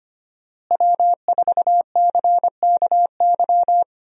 Morse code, heard as a keyed tone, about 25 wpm, 700 Hz. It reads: W4CKY
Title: USA Call Signs - 5 Characters - Repeated 3x - 25wpm